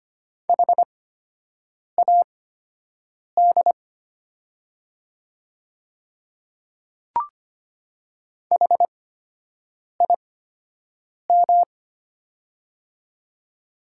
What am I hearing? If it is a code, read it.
Morse code, heard as a keyed tone, about 25 wpm, spaced as for 7 wpm, 700 Hz. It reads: HAD HIM
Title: 8x Character Spacing - Top 100 Words - 25wpm